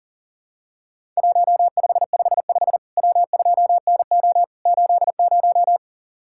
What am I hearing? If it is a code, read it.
Morse code, heard as a keyed tone, about 40 wpm, 700 Hz. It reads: 1555 W2NO 80
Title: Worked All Europe DX Contest - QTC Exchange - 40wpm